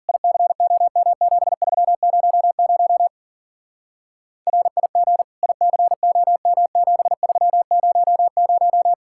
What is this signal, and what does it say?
Morse code, heard as a keyed tone, about 50 wpm, 700 Hz. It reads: ICOM7300 RIG ICOM7300